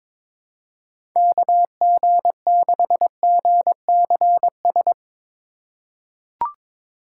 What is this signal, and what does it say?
Morse code, heard as a keyed tone, about 23 wpm, 700 Hz. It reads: KG6GCS
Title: USA Call Signs - 6 Characters - 3x Character Spacing - 22wpm